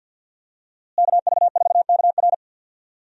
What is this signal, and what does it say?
Morse code, heard as a keyed tone, about 48 wpm, 700 Hz. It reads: KV4KR
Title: USA Call Signs - 5 Characters - Repeated 3x - 50wpm